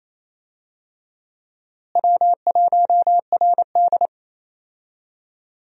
Morse code, heard as a keyed tone, about 28 wpm, 700 Hz. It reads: W1RD
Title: USA Call Signs - 4 Characters - Repeated 2x - 28wpm